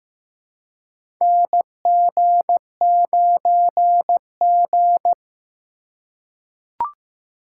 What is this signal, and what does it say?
Morse code, heard as a keyed tone, about 15 wpm, 700 Hz. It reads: NG9G
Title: USA Call Signs - 4 Characters - 3x Character Spacing - 15wpm